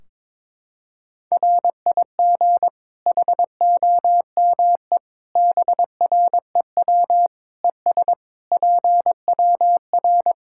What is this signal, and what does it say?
Morse code, heard as a keyed tone, about 22 wpm, 700 Hz. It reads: RIG HOME BREW ES PWR
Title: Practice QSO - Exchange 2 - Info - 22wpm